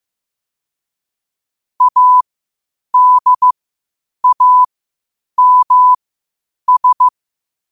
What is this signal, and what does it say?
Morse code, heard as a keyed tone, about 15 wpm, spaced as for 8 wpm, 1000 Hz. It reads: ADAMS
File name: qotdfiles / 8wpm